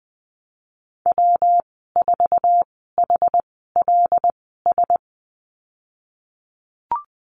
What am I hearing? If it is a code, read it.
Morse code, heard as a keyed tone, about 20 wpm, spaced as for 14 wpm, 700 Hz. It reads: W4HLS
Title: 140 Lesson 23g - Letter f - Call Signs 20wpm